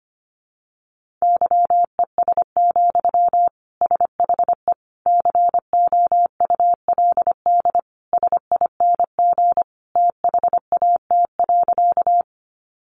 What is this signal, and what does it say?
Morse code, heard as a keyed tone, about 25 wpm, 700 Hz. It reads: YES, SHE COULD SING THAT.
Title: Sentences from Top 500 Words 25wpm